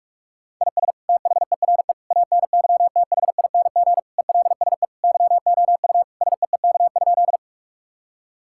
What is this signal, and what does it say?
Morse code, heard as a keyed tone, about 46 wpm, 700 Hz. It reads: IS THERE ANYTHING ELSE YOU SEEK?
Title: Sentences from Top 1600 Words 45wpm